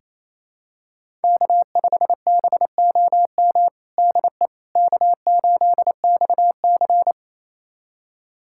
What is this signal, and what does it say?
Morse code, heard as a keyed tone, about 27 wpm, 700 Hz. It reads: K5BOM DE K8XC
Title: USA Callsign DE Callsign 28wpm